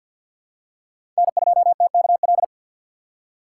Morse code, heard as a keyed tone, about 51 wpm, 700 Hz. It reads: N2TKL
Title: USA Call Signs - 5 Characters - 3x Character Spacing - 50wpm